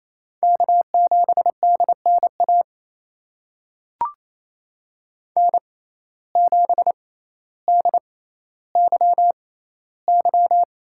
Morse code, heard as a keyed tone, about 28 wpm, spaced as for 14 wpm, 700 Hz. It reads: K7DNA N7DYY
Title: USA Call Signs - 5 Characters - 6x Character Spacing - 28wpm